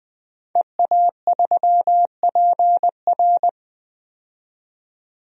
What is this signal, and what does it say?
Morse code, heard as a keyed tone, about 20 wpm, 700 Hz. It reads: EA3PR
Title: DXCC-100 Call Signs - 5 Characters - 20wpm